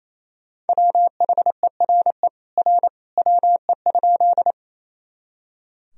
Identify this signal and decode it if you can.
Morse code, heard as a keyed tone, about 28 wpm, 700 Hz. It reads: WHERE R WE?
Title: Sentences from Top 200 Words and Abbreviations 28wpm